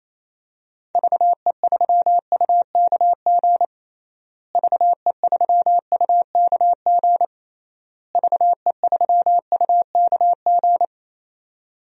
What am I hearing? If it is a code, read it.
Morse code, heard as a keyed tone, about 28 wpm, 700 Hz. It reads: VE3UKG VE3UKG VE3UKG
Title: DXCC-100 Call Signs - 6 Characters - Repeated 3x - 28wpm